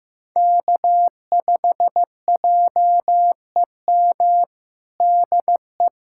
Morse code, heard as a keyed tone, about 15 wpm, 700 Hz. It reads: K5JEM DE
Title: USA Callsign DE Callsign 15wpm